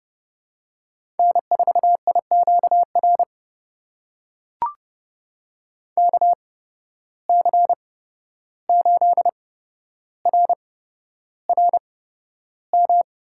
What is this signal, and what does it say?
Morse code, heard as a keyed tone, about 30 wpm, spaced as for 12 wpm, 700 Hz. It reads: N4IQR KC8RRM
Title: USA Call Signs - 8x Character Spacing - 30wpm